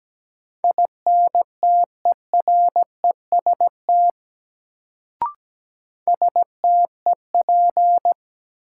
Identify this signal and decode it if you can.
Morse code, heard as a keyed tone, about 17 wpm, 700 Hz. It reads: INTEREST STEP
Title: Sets of 5 Words Encore - New Word Drill - 17wpm